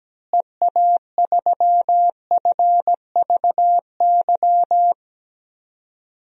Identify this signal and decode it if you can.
Morse code, heard as a keyed tone, about 17 wpm, 700 Hz. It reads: EA3FVY
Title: DXCC-100 Call Signs - 6 Characters - Repeated 3x - 17wpm